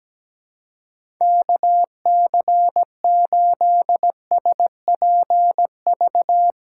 Morse code, heard as a keyed tone, about 17 wpm, 700 Hz. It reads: KC8SPV
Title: USA Call Signs - 6x Character Spacing - 17wpm